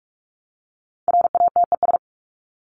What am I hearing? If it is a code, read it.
Morse code, heard as a keyed tone, about 44 wpm, 700 Hz. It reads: RATES
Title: ES:S Suffix - Top 100 Words with Base Words 45wpm